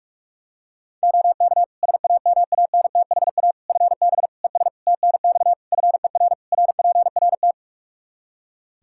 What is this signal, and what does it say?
Morse code, heard as a keyed tone, about 41 wpm, 700 Hz. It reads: OK SAMANTHA FB ES TNX FER RPRT
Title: Practice QSO - Exchange 2 - Info - 45wpm